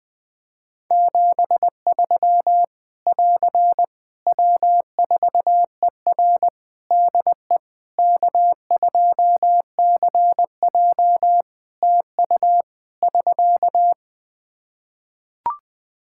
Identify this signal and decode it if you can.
Morse code, heard as a keyed tone, about 20 wpm, 700 Hz. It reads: 73 <AR> W4ER DE K2CJ TU <SK>
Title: Practice QSO - Exchange 4 - Ending - 20wpm